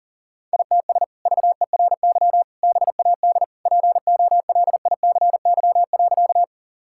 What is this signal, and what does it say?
Morse code, heard as a keyed tone, about 37 wpm, 700 Hz. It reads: ITS VERY BAD POLICY.
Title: Sentences from Top 1000 Words - Review Part 5 - 40wpm